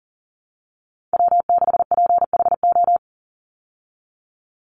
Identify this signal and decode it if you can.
Morse code, heard as a keyed tone, about 40 wpm, 700 Hz. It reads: W6PHO
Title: Call Signs Repeated 3x - 40wpm